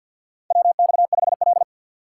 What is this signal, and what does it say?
Morse code, heard as a keyed tone, about 52 wpm, 700 Hz. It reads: WX5L
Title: USA Call Signs - 4x Character Spacing - 50wpm